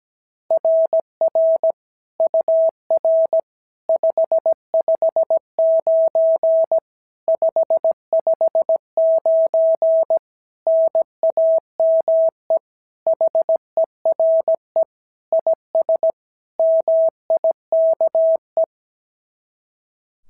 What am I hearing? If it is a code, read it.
Morse code, heard as a keyed tone, about 17 wpm, 690 Hz. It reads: RR UR 559 559 NAME HERE IS MIKE
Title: SOTA QSO Practice 17wpm